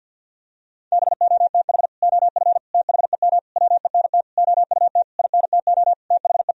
Morse code, heard as a keyed tone, about 50 wpm, 700 Hz. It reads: BOTH OF THEM WENT OUT INTO THE
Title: Sentences from Top 1000 Words - Review Part 2 - 50wpm